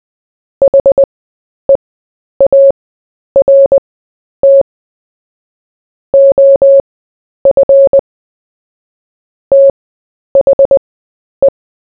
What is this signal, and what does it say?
Morse code, heard as a keyed tone, about 20 wpm, spaced as for 10 wpm, 550 Hz. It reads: HEART OF THE
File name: cw / fists / Short-Story-4-10wpm